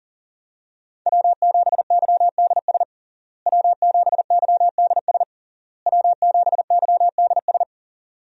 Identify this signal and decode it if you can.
Morse code, heard as a keyed tone, about 40 wpm, 700 Hz. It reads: W7YDS W7YDS W7YDS
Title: USA Call Signs - 5 Characters - Repeated 3x - 40wpm